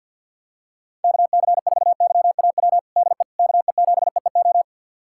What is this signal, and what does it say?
Morse code, heard as a keyed tone, about 52 wpm, 700 Hz. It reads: KC4YAW DE KE7EEO
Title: USA Callsign DE Callsign 50wpm